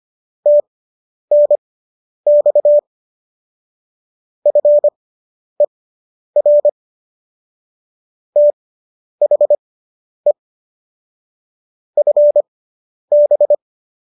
Morse code, heard as a keyed tone, about 25 wpm, spaced as for 10 wpm, 590 Hz. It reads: TNX FER THE FB